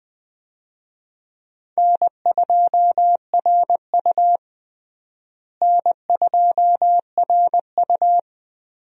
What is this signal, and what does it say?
Morse code, heard as a keyed tone, about 20 wpm, 700 Hz. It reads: N2RU N2RU
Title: USA Call Signs - 4 Characters - Repeated 3x - 20wpm